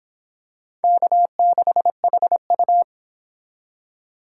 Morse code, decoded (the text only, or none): K6HU